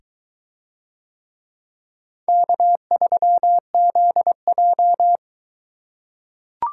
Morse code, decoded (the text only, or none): K3ZJ